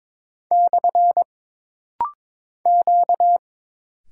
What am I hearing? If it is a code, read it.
Morse code, heard as a keyed tone, about 22 wpm, 700 Hz. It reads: / Q
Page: This rapid-fire timing is geared to push your ICR (Instant Character Recognition) skills.